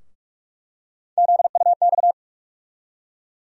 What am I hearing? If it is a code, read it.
Morse code, heard as a keyed tone, about 46 wpm, 700 Hz. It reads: ZUX
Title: USA Call Sign Postfixes - 3 Letters 45wpm